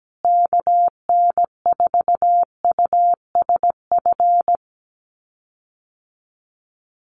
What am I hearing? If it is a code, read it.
Morse code, heard as a keyed tone, about 17 wpm, 700 Hz. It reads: KN4USF
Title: USA Call Signs Repeated 2x 17wpm